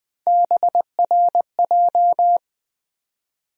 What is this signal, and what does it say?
Morse code, heard as a keyed tone, about 20 wpm, 700 Hz. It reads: BRJ
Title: USA Call Sign Postfixes - 3 Letters 20wpm